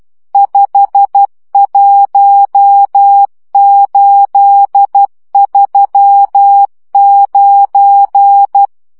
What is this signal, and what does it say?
Morse code, heard as a keyed tone, about 12 wpm, 800 Hz. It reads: 51839
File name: morze vizsga 2011-3 10